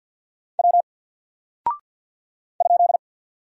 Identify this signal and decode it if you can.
Morse code, heard as a keyed tone, about 50 wpm, 700 Hz. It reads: W ?
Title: Single Character (Letter:Number:Punctuation:Prosign) 50wpm